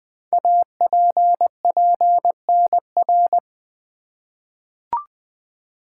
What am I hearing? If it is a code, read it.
Morse code, heard as a keyed tone, about 21 wpm, 700 Hz. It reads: APPNR
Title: Code Groups - 5 Letter:Numbers - 20wpm